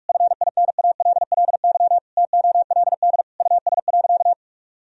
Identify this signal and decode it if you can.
Morse code, heard as a keyed tone, about 46 wpm, 700 Hz. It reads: FINALLY TOLD US.
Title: Sentences from Top 1000 Words - Review Part 3 - 45wpm